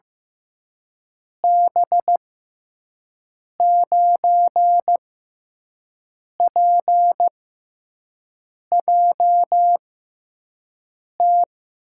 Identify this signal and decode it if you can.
Morse code, heard as a keyed tone, about 15 wpm, spaced as for 5 wpm, 700 Hz. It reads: B9PJT
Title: USA Call Signs - 6x Character Spacing - 15wpm